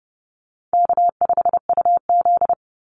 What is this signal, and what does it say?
Morse code, heard as a keyed tone, about 30 wpm, 700 Hz. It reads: K5UZ
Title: Easy Call Signs Repeated 3x - 30wpm